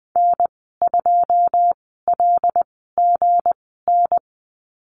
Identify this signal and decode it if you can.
Morse code, heard as a keyed tone, about 19 wpm, spaced as for 14 wpm, 700 Hz. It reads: N2LGN
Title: 207 Lesson 33g - Letter j - Call Signs 20wpm